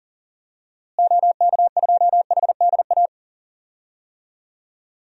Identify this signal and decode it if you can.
Morse code, heard as a keyed tone, about 40 wpm, 700 Hz. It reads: OK2HDA
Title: DXCC-100 Call Signs - 6 Characters - Repeated 3x - 40wpm